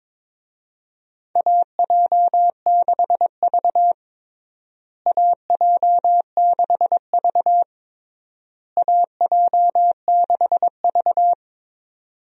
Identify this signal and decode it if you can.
Morse code, heard as a keyed tone, about 22 wpm, 700 Hz. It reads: AJ6V AJ6V AJ6V
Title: USA Call Signs - 4 Characters - Repeated 3x - 22wpm